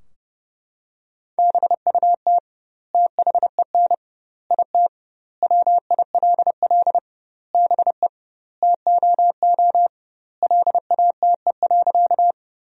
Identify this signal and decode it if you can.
Morse code, heard as a keyed tone, about 30 wpm, 700 Hz. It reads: BUT THEN IT WILL BE TOO LATE.
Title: 2x Word-Spacing - Sentences from Top 300 Words - 30wpm